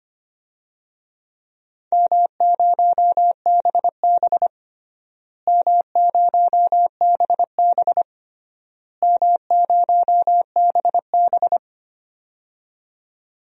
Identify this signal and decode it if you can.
Morse code, heard as a keyed tone, about 25 wpm, 700 Hz. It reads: M0BB M0BB M0BB
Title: DXCC-100 Call Signs - 4 Characters - Repeated 3x - 25wpm